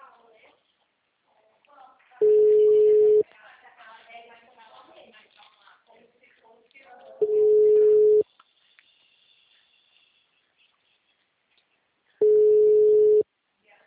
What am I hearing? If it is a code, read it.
Ringback tone (Europe, 425 Hz), with speech.